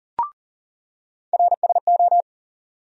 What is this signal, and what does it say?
Morse code, heard as a keyed tone, about 40 wpm, 700 Hz. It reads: RSO
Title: USA Call Sign Postfixes - 3 Letters 40wpm